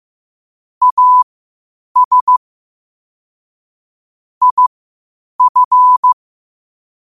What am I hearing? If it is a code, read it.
Morse code, heard as a keyed tone, about 15 wpm, spaced as for 8 wpm, 1000 Hz. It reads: AS IF
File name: qotdfiles / 8wpm